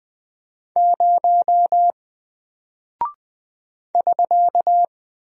Morse code, heard as a keyed tone, about 20 wpm, 700 Hz. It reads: 0 <SK>
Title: Single Character (Letter:Number:Punctuation:Prosign) Alt 20wpm